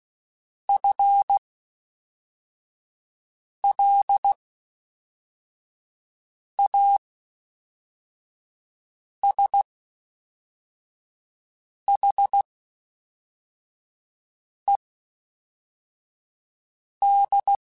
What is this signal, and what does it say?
Morse code, heard as a keyed tone, about 16 wpm, spaced as for 4 wpm, 800 Hz. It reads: FLASHED
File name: Spaced / FL Runs sp mp3s / FL Runs sp @5